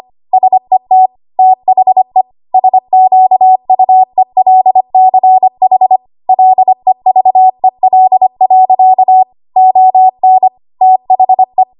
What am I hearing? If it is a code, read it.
Morse code, heard as a keyed tone, about 25 wpm, 750 Hz. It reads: SET THE SQUELCH LEVEL. ON THE
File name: Archive / 25 WPM / 25WPM